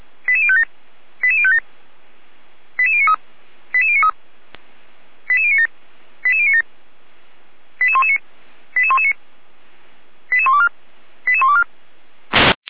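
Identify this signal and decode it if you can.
Five-tone selective call (ZVEI), heard as quick runs of five tones, tones of about 70 ms. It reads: 88057 88057 88093 88093 88097 88097 88119 88119 88124 88124
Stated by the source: ZVEI five-tone selective call received via SDR, recorded by Feuerwehr Orlamünde.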